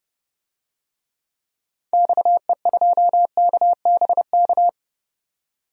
Morse code, heard as a keyed tone, about 32 wpm, 700 Hz. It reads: XE2KBK
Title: DXCC-100 Call Signs - 6 Characters - 30wpm